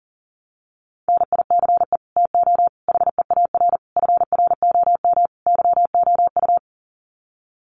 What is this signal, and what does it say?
Morse code, heard as a keyed tone, about 40 wpm, 700 Hz. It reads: NICE TO HEAR FROM YOU
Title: Common QSO Phrases 40wpm